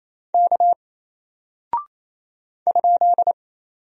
Morse code, heard as a keyed tone, about 28 wpm, 700 Hz. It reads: K ?
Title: Single Character (Letter:Number:Punctuation:Prosign) Alt 28wpm